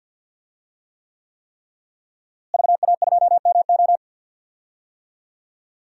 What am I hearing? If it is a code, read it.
Morse code, heard as a keyed tone, about 51 wpm, 700 Hz. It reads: VA2MO
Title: DXCC-100 Call Signs - 5 Characters - Repeated 3x - 50wpm